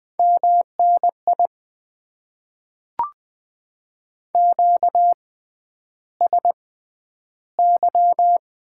Morse code, heard as a keyed tone, about 20 wpm, spaced as for 10 wpm, 700 Hz. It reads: MNI QSY
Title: Top 90 QSO Elements - 6x Character Spacing - 20wpm